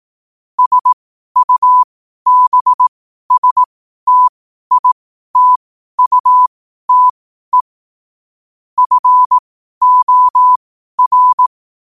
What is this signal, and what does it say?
Morse code, heard as a keyed tone, about 18 wpm, spaced as for 12 wpm, 1000 Hz. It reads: SUBSTITUTE FOR
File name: qotdfiles / 12wpm